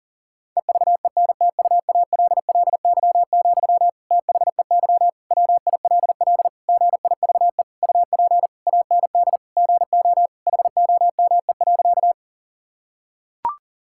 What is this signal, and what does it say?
Morse code, heard as a keyed tone, about 40 wpm, 700 Hz. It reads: EVENTUALLY, THEY WILL GIVE UP AND GO HOME.
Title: Sentences from Top 1000 Words - Review Part 6 - 40wpm